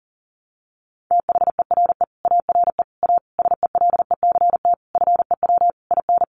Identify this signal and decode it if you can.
Morse code, heard as a keyed tone, about 40 wpm, 700 Hz. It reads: THERE ARE A SELECT FEW IN
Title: Sentences from Top 1300 Words 40wpm